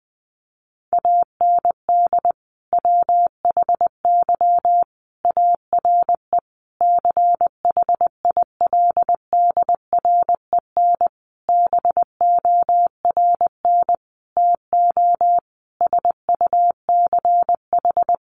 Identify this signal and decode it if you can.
Morse code, heard as a keyed tone, about 20 wpm, 700 Hz. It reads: AND WHY ARE CHILDREN BORN TO SUCH
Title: Sentences from Top 1000 Words 20wpm